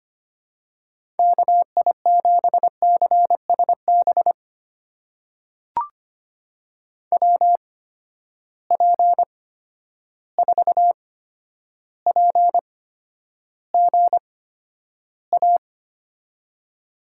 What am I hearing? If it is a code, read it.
Morse code, heard as a keyed tone, about 25 wpm, spaced as for 11 wpm, 700 Hz. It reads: KI7CSB WP4PGA
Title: USA Call Signs - 6 Characters - 8x Character Spacing - 25wpm